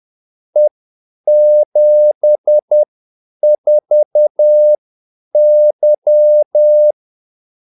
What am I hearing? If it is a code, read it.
Morse code, heard as a keyed tone, about 10 wpm, spaced as for 8 wpm, 600 Hz. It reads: E74Y
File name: Callsigns-10-8wpm